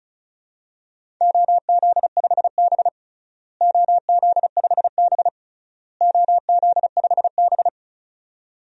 Morse code, heard as a keyed tone, about 34 wpm, 700 Hz. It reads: OZ5B OZ5B OZ5B
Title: DXCC-100 Call Signs - 4 Characters - Repeated 3x - 35wpm